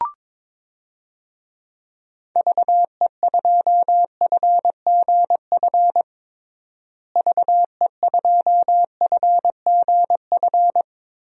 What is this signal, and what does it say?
Morse code, heard as a keyed tone, about 22 wpm, 700 Hz. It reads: VE2FGF VE2FGF
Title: DXCC-100 Call Signs - 6 Characters - Repeated 2x - 22wpm